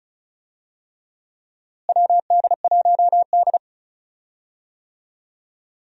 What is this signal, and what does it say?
Morse code, heard as a keyed tone, about 36 wpm, 700 Hz. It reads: WD1D
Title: USA Call Signs - 4 Characters - Repeated 3x - 35wpm